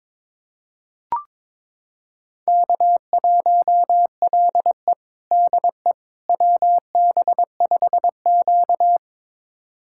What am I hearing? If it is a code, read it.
Morse code, heard as a keyed tone, about 21 wpm, 700 Hz. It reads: K1LE DE WB5Q
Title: USA Callsign DE Callsign - 4 Characters - 22wpm